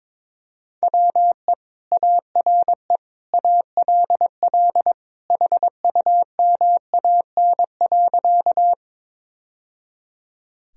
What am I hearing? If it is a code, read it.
Morse code, heard as a keyed tone, about 22 wpm, 700 Hz. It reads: WE ARE ALL HUMAN.
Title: Sentences from Top 1000 Words - Review Part 3 - 22wpm